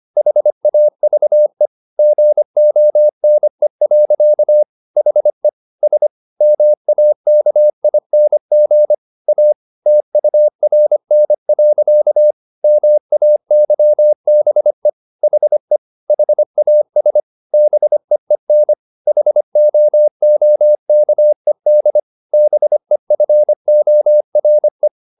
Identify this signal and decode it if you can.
Morse code, heard as a keyed tone, about 25 wpm, 600 Hz. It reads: HAVE GONE. HE S MAKING A TURN. MAYBE HE HAS BEEN HOOKED BEFORE
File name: the old man and the sea part 1 25wpm